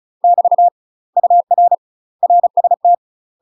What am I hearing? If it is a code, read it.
Morse code, heard as a keyed tone, about 34 wpm, 700 Hz. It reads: <BT> UR RST
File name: UK MASTER QSOs TRAINING / 35WPM / UK MASTER QSOs 0039